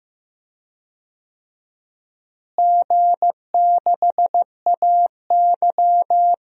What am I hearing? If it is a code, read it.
Morse code, heard as a keyed tone, about 15 wpm, 700 Hz. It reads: G6AY
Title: DXCC-100 Call Signs - 4 Characters - Repeated 3x - 15wpm